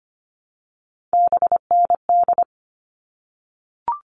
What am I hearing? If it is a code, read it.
Morse code, heard as a keyed tone, about 25 wpm, 700 Hz. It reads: BND
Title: All Cut Numbers - 3-Digit 25wpm